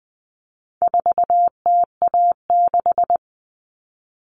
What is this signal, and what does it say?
Morse code, heard as a keyed tone, about 20 wpm, 700 Hz. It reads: 4TA6
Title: All Cut Numbers - 4-Digit 20wpm